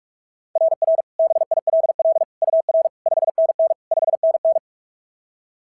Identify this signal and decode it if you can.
Morse code, heard as a keyed tone, about 44 wpm, 660 Hz. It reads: RR BILL UR 5NN 5NN
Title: SOTA QSO Practice 45wpm